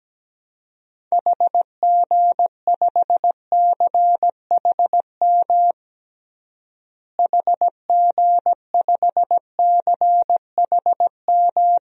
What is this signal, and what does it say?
Morse code, heard as a keyed tone, about 17 wpm, 700 Hz. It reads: HG5CHM HG5CHM
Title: DXCC-100 Call Signs - 6 Characters - Repeated 3x - 17wpm